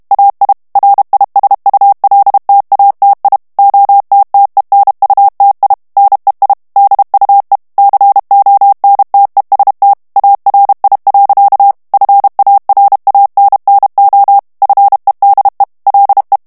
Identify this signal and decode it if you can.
Morse code, heard as a keyed tone, about 32 wpm, 800 Hz. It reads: AI RISULTATI OTTENUTI NEI DUE CONTEST ARI. FARANNO FEDE LE
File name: ITALIEN texte clair 2 vitesse 30 mots minute